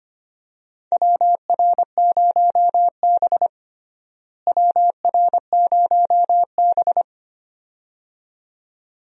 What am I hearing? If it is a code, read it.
Morse code, heard as a keyed tone, about 25 wpm, 700 Hz. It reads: WR0B WR0B
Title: USA Call Signs - 4 Characters - Repeated 2x - 25wpm